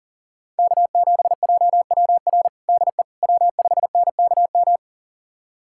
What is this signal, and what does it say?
Morse code, heard as a keyed tone, about 37 wpm, 700 Hz. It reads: K7JWR DE W5NKM
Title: USA Callsign DE Callsign 40wpm